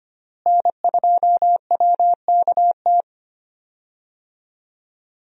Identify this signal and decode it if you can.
Morse code, heard as a keyed tone, about 25 wpm, 700 Hz. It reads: N2WKT
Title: USA Call Signs - 5 Characters - Repeated 3x - 25wpm